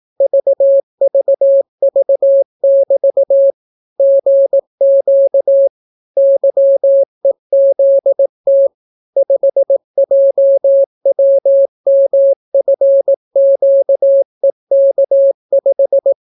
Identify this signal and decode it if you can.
Morse code, heard as a keyed tone, about 18 wpm, 550 Hz. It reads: VVV<BT> GQ YEZT 5JWMFQEK5